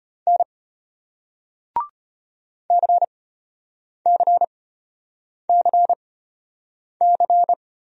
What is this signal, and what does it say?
Morse code, heard as a keyed tone, about 33 wpm, spaced as for 8 wpm, 700 Hz. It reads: N CCCC